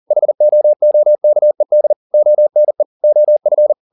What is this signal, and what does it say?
Morse code, heard as a keyed tone, about 40 wpm, 600 Hz. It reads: HOOKED ONE OF